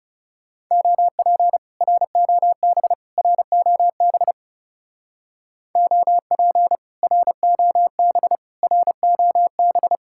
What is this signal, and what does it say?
Morse code, heard as a keyed tone, about 37 wpm, 700 Hz. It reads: OP ROB ROB OP ROB ROB
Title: Speed-Racing Common QSO Phrases Encore - 30wpm